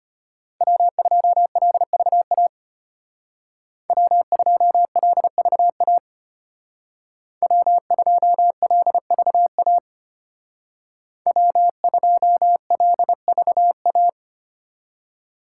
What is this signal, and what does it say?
Morse code, heard as a keyed tone, about 31 wpm, 700 Hz. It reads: W2LVA W2LVA W2LVA W2LVA
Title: Speed Racing - USA 1x3 Call Signs 25wpm